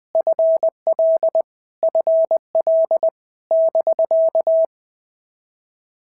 Morse code, heard as a keyed tone, about 20 wpm, 650 Hz. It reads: FL FL <BK>
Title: POTA QSO Practice - 20wpm